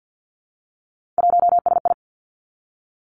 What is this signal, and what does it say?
Morse code, heard as a keyed tone, about 50 wpm, 700 Hz. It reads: 1SI
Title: Three Challenging Letter-Number Combinations 50wpm